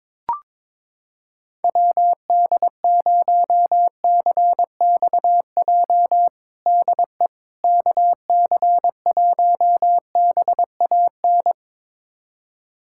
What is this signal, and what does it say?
Morse code, heard as a keyed tone, about 22 wpm, 700 Hz. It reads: WD0CXJ DE KC1BAN
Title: USA Callsign DE Callsign - 6 Characters - 22wpm